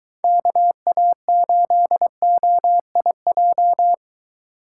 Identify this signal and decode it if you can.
Morse code, heard as a keyed tone, about 22 wpm, 700 Hz. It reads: KA8OIJ